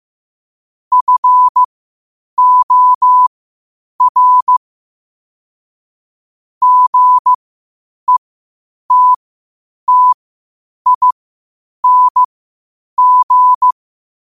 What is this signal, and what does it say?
Morse code, heard as a keyed tone, about 15 wpm, spaced as for 8 wpm, 1000 Hz. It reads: FOR GETTING
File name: 8wpm